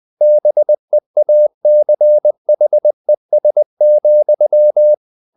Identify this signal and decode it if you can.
Morse code, heard as a keyed tone, about 20 wpm, 600 Hz. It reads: BEACHES,
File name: the old man and the sea / the old man and the sea part 1 20wpm